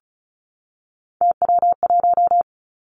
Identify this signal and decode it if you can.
Morse code, heard as a keyed tone, about 36 wpm, 700 Hz. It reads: TW1
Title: Three Challenging Letter-Number Combinations 35wpm